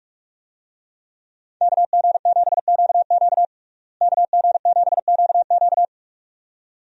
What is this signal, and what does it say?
Morse code, heard as a keyed tone, about 41 wpm, 700 Hz. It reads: KG7QQ KG7QQ
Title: USA Call Signs - 5 Characters - Repeated 2x - 45wpm